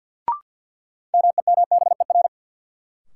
Morse code, heard as a keyed tone, about 53 wpm, 700 Hz. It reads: MEMBER
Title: Sentences from Top 1100 Words - New Word Drill Alt - 50wpm